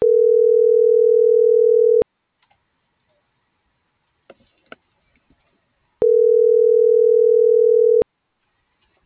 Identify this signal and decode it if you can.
Ringback tone (North America), 440 + 480 Hz.
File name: CENTRAL GENERAL / out alexia.jauregui